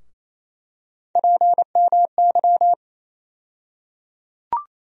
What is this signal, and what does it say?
Morse code, heard as a keyed tone, about 28 wpm, 700 Hz. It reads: PMY
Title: USA Call Sign Postfixes - 3 Letters 28wpm